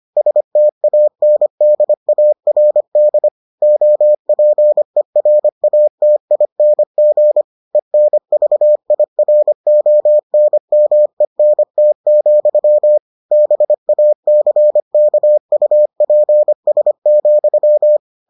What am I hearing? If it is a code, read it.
Morse code, heard as a keyed tone, about 25 wpm, 600 Hz. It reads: STANDARD OPERATING ENVIRONMENT, BACKUPS,